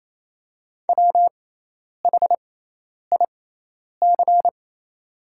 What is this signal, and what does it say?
Morse code, heard as a keyed tone, about 28 wpm, spaced as for 10 wpm, 700 Hz. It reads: WHIC
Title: 6x Character Spacing - Top 100 Words - 28wpm